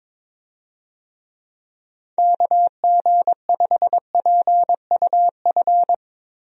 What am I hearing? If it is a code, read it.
Morse code, heard as a keyed tone, about 22 wpm, 700 Hz. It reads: KG5PUF
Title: USA Call Signs - 6 Characters - Repeated 2x - 22wpm